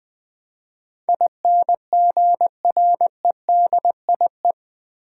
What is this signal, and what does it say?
Morse code, heard as a keyed tone, about 20 wpm, 700 Hz. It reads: INGREDIE
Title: Build A Word - 4 Letters At A Time - 10-Letter Words - 20wpm